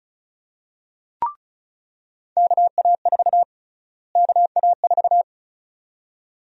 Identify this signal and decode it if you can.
Morse code, heard as a keyed tone, about 34 wpm, 700 Hz. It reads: KA4 KA4
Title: USA Call Sign Prefixes - Repeated 2x - 35wpm